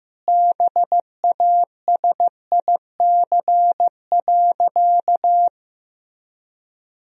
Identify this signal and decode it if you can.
Morse code, heard as a keyed tone, about 16 wpm, 700 Hz. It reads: BASIC.
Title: Sentences from Top 1000 Words - Review Part 3 - 15wpm